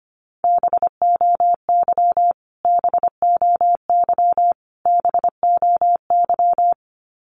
Morse code, heard as a keyed tone, about 25 wpm, 700 Hz. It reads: BOY BOY BOY
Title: Top 300 Words Repeated 25wpm